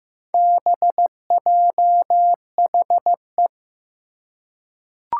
Morse code, heard as a keyed tone, about 15 wpm, 700 Hz. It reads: BJHE
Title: Four Challenging Letter-Number Combinations 15wpm